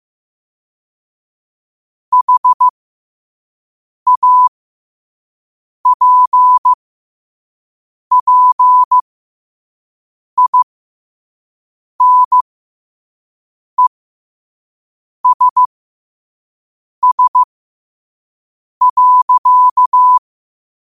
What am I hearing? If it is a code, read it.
Morse code, heard as a keyed tone, about 15 wpm, spaced as for 5 wpm, 1000 Hz. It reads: HAPPINESS.
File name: qotdfiles / 5wpm